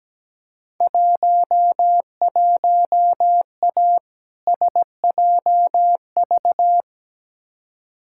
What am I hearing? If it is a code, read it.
Morse code, heard as a keyed tone, about 17 wpm, 700 Hz. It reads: 11A SJV
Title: Field Day Exchanges Encore - 17wpm